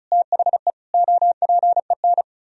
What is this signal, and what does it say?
Morse code, heard as a keyed tone, about 36 wpm, 700 Hz. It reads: THE OPEN
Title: Sentences from Top 1000 Words - Review Part 1 - 35wpm